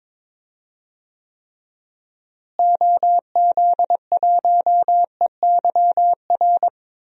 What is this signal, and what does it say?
Morse code, heard as a keyed tone, about 22 wpm, 700 Hz. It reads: OZ1EYR
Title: DXCC-100 Call Signs - 6 Characters - Repeated 2x - 22wpm